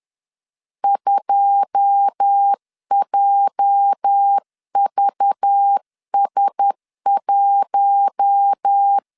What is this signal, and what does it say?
Morse code, heard as a keyed tone, about 10 wpm, 800 Hz. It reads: 2JVS1
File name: LEZIONI / 9 286 10wpm 2735